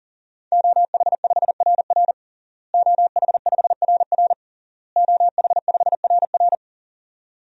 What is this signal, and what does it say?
Morse code, heard as a keyed tone, about 37 wpm, 700 Hz. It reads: OH5RR OH5RR OH5RR
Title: DXCC-100 Call Signs - 5 Characters - Repeated 3x - 40wpm